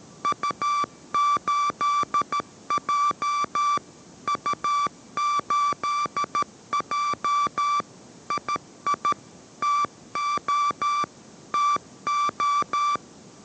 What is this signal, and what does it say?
Morse code, heard as a keyed tone, about 13 wpm, 1160 Hz. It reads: U8J U8J II TO TO